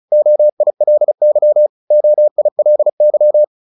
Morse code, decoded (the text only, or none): OILY OILY